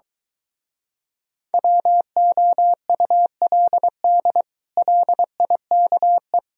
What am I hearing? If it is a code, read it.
Morse code, heard as a keyed tone, about 24 wpm, 700 Hz. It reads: WOULD LIKE